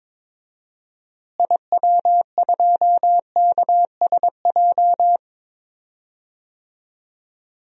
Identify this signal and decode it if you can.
Morse code, heard as a keyed tone, about 22 wpm, 700 Hz. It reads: IW2KSJ
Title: DXCC-100 Call Signs - 6 Characters - Repeated 2x - 22wpm